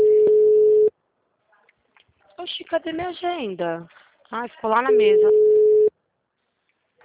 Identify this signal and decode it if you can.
Ringback tone (Europe, 425 Hz), with speech.
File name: out- -2026- - - .188